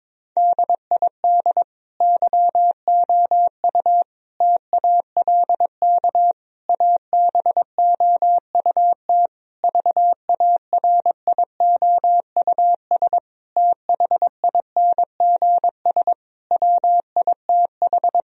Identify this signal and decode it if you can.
Morse code, heard as a keyed tone, about 22 wpm, 700 Hz. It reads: DID YOU TALK ABOUT VARIOUS THINGS WITH
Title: Sentences from Top 1000 Words - Review Part 4 - 22wpm